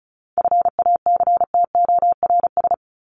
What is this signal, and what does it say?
Morse code, heard as a keyed tone, about 34 wpm, 700 Hz. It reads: FACTORS
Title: ES:S Suffix - Top 100 Words with Base Words 35wpm